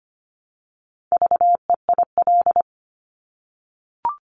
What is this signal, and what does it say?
Morse code, heard as a keyed tone, about 25 wpm, 700 Hz. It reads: VEIL
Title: Multi-letter Phonemes Repeated 3x 25wpm